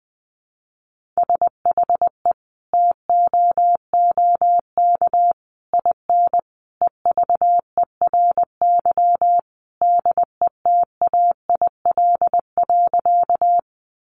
Morse code, heard as a keyed tone, about 20 wpm, 700 Hz. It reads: SHE TOOK IN EVERY DETAIL.
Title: Sentences from Top 1500 Words 20wpm